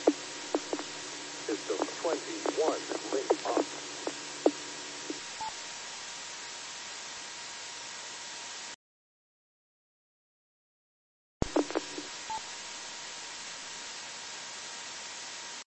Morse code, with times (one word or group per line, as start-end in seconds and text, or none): none